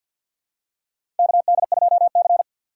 Morse code, heard as a keyed tone, about 50 wpm, 700 Hz. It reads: KD2C
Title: USA Call Signs - 4 Characters - 6x Character Spacing - 50wpm